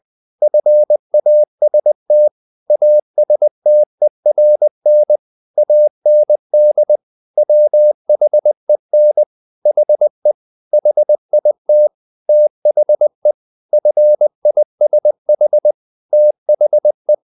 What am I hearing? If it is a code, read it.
Morse code, heard as a keyed tone, about 20 wpm, 600 Hz. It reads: FAST ASTERN AND WHEN HE HIT THE FISH THE
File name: the old man and the sea part 2 20wpm